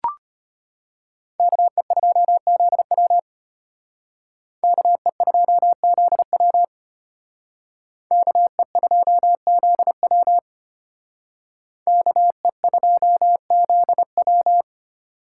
Morse code, heard as a keyed tone, about 32 wpm, 700 Hz. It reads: KE2ZW KE2ZW KE2ZW KE2ZW